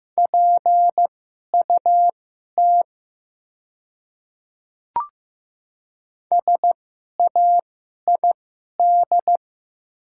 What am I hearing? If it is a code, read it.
Morse code, heard as a keyed tone, about 15 wpm, spaced as for 11 wpm, 700 Hz. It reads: PUT SAID
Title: 2x Character Spacing - Top 200 Words - 15wpm